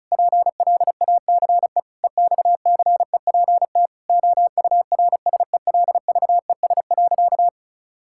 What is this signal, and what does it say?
Morse code, heard as a keyed tone, about 36 wpm, 700 Hz. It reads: PLACE EXCEPT OURSELVES.
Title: Sentences from Top 1600 Words 35wpm